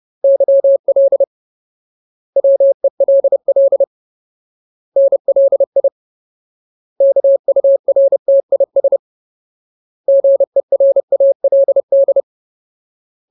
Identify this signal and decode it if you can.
Morse code, heard as a keyed tone, about 30 wpm, 550 Hz. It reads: YL WELL NLI KURTIS GERALD
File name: Ultimate Word List / 30WPM Ultimate